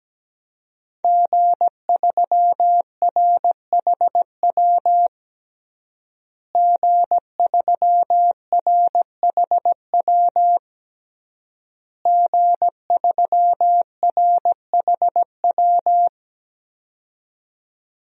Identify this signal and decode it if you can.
Morse code, heard as a keyed tone, about 17 wpm, 700 Hz. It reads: G3RHW G3RHW G3RHW
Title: DXCC-100 Call Signs - 5 Characters - Repeated 3x - 17wpm